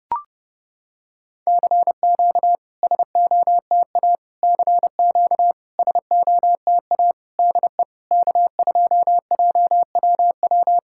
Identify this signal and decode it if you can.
Morse code, heard as a keyed tone, about 30 wpm, 700 Hz. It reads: CQ SOTA CQ SOTA DE K2JWW
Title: SOTA QSO Practice 30wpm